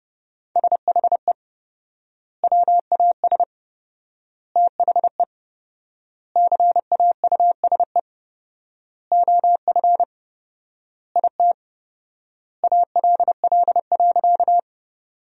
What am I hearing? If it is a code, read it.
Morse code, heard as a keyed tone, about 30 wpm, 700 Hz. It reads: SHE WAS THE CAUSE OF IT ALL.
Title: 4x Word-Spacing - Sentences from Top 200 Words - 30wpm